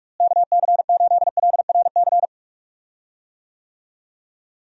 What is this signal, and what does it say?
Morse code, heard as a keyed tone, about 42 wpm, 700 Hz. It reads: KC8LRC
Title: USA Call Signs - 6 Characters - Repeated 3x - 45wpm